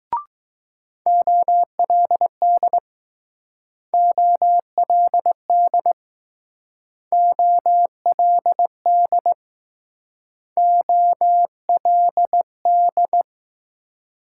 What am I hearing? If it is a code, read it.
Morse code, heard as a keyed tone, about 20 wpm, 700 Hz. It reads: OLD OLD OLD OLD